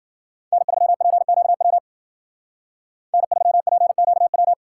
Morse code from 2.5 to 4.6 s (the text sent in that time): N3PQW